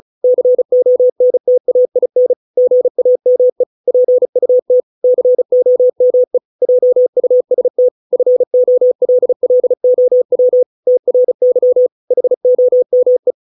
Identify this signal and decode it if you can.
Morse code, heard as a keyed tone, about 36 wpm, 500 Hz. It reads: CONTAIN GAME PUT COME JUST FOLLOW TRY HOME